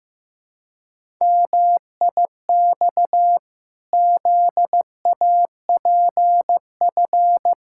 Morse code, heard as a keyed tone, about 15 wpm, 700 Hz. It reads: MIX ZAPF
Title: Pangram Challenge v2 - 15wpm